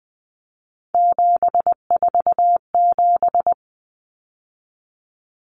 Morse code, heard as a keyed tone, about 20 wpm, 700 Hz. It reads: 747
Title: Numbers - 3-Digit 20wpm